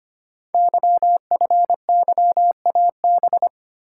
Morse code, heard as a keyed tone, about 25 wpm, 700 Hz. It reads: YFYAB
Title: Code Groups - 5 Letter:Numbers - 25wpm